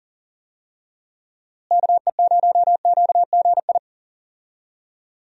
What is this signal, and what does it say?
Morse code, heard as a keyed tone, about 40 wpm, 700 Hz. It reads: KE0QGI
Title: USA Call Signs - 6 Characters - Repeated 2x - 40wpm